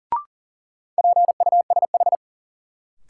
Morse code, heard as a keyed tone, about 37 wpm, 700 Hz. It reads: PUSH
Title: Sentences from Top 1300 Words - New Word Drill Alt - 40wpm